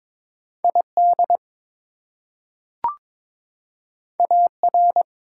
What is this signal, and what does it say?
Morse code, heard as a keyed tone, about 22 wpm, 700 Hz. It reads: ID AR
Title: US States and Canadian Provinces 22wpm